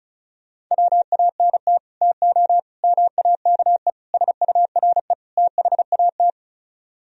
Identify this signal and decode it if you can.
Morse code, heard as a keyed tone, about 46 wpm, 700 Hz. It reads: WANT TO MAKE SURE THAT